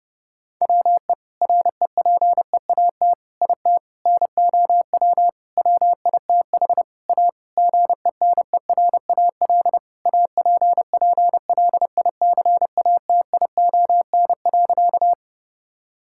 Morse code, heard as a keyed tone, about 32 wpm, 700 Hz. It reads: WE REPEAT IT NOW WITH A GENERAL APPLICATION.
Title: Sentences from Top 1600 Words 30wpm